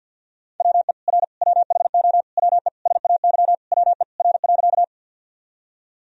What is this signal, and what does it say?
Morse code, heard as a keyed tone, about 46 wpm, 700 Hz. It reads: WE R WHO WE SAY WE R.
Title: Sentences from Top 200 Words and Abbreviations 50wpm